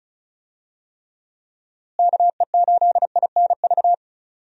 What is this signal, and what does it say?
Morse code, heard as a keyed tone, about 36 wpm, 700 Hz. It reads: KE8INV
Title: USA Call Signs - 6 Characters - Repeated 3x - 35wpm